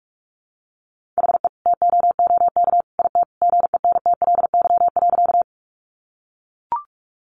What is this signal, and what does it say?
Morse code, heard as a keyed tone, about 44 wpm, 700 Hz. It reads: HE TOOK IT GENTLY.
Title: Sentences from Top 1500 Words 45wpm